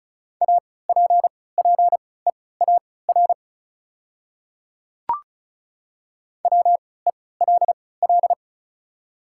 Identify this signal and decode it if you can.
Morse code, heard as a keyed tone, about 34 wpm, spaced as for 20 wpm, 690 Hz. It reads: APPEAR WELL
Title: 3x Character Spacing - Top 500 Words - 35wpm